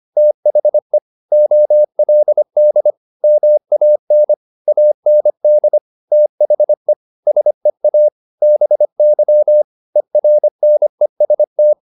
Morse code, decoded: THE OLD MAN AND THE SEA BY ERNEST

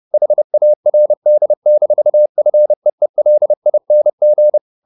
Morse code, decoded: HARD-FEELING

